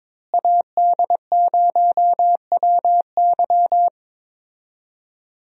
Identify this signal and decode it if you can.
Morse code, heard as a keyed tone, about 21 wpm, 700 Hz. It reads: AD0WY
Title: USA Call Signs - 5 Characters - Repeated 3x - 22wpm